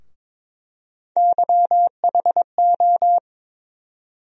Morse code, heard as a keyed tone, about 22 wpm, 700 Hz. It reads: YHO
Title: USA Call Sign Postfixes - 3 Letters 22wpm